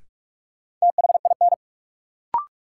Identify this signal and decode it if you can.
Morse code, heard as a keyed tone, about 46 wpm, 700 Hz. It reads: THIN
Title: Sentences from Top 1200 Words - New Word Drill Alt - 45wpm